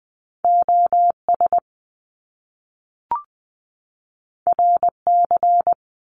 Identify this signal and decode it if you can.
Morse code, heard as a keyed tone, about 20 wpm, 700 Hz. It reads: O S R C
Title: Top Two Letter Combinations in Words 20wpm